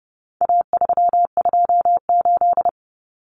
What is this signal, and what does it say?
Morse code, heard as a keyed tone, about 29 wpm, 700 Hz. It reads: A328
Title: Common Cut Numbers - 4-Digit 30wpm